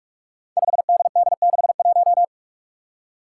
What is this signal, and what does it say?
Morse code, heard as a keyed tone, about 46 wpm, 700 Hz. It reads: 5DD61
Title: Five Challenging Letter-Number Combinations 45wpm